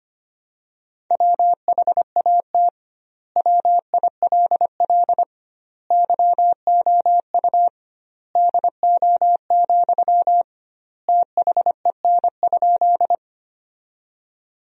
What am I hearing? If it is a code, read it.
Morse code, heard as a keyed tone, about 25 wpm, 700 Hz. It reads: WHAT WILL YOU DO, THEN?
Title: 2x Word-Spacing - Sentences from Top 100 Words - 25wpm